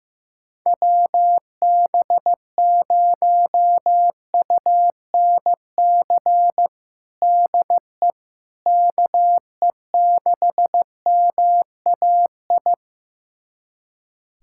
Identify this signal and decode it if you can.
Morse code, heard as a keyed tone, about 15 wpm, 700 Hz. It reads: WB0UNC DE KE6MAI
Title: USA Callsign DE Callsign 15wpm